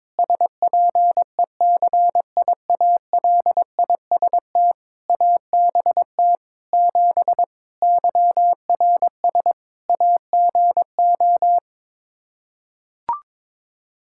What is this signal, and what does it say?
Morse code, heard as a keyed tone, about 21 wpm, 700 Hz. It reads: SPECIALIST ABT 7 YRS AGO
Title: Practice QSO - Exchange 3 - More Info - 22wpm